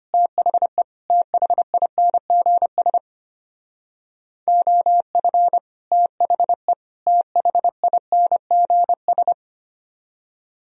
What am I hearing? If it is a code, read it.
Morse code, heard as a keyed tone, about 30 wpm, 700 Hz. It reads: THE THINGS OF THE THINGS
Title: Speed-Racing - Sets of 3 Words 25wpm